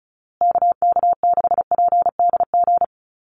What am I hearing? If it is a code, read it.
Morse code, heard as a keyed tone, about 34 wpm, 700 Hz. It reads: KK6PDG
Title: USA Call Signs Repeated 3x 35wpm